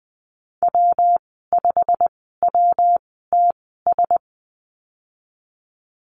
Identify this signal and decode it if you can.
Morse code, heard as a keyed tone, about 20 wpm, spaced as for 15 wpm, 700 Hz. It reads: W5WTS
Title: 122 Lesson 20g - Number 3 - Call Signs 20wpm